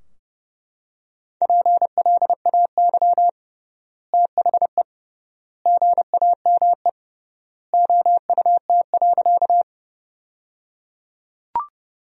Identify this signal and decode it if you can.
Morse code, heard as a keyed tone, about 29 wpm, 700 Hz. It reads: PLAY THE GAME OUT.
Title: 3x Word-Spacing - Sentences from Top 500 Words - 30wpm